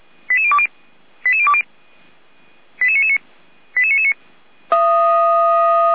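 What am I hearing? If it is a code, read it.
Five-tone selective call (ZVEI), heard as quick runs of five tones, tones of about 70 ms. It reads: 88020 88020 88999 88999
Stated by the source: ZVEI five-tone call received by SDR.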